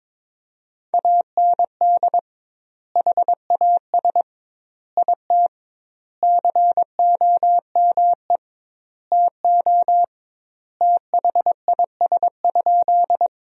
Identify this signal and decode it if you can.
Morse code, heard as a keyed tone, about 21 wpm, 700 Hz. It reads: AND HAS IT COME TO THIS?
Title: 2x Word-Spacing - Sentences from Top 100 Words - 22wpm